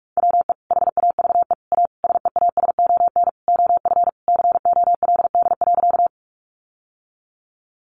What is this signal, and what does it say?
Morse code, heard as a keyed tone, about 44 wpm, 700 Hz. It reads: WE HAVE A SEASON OF COLD.
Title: Sentences from Top 1400 Words 45wpm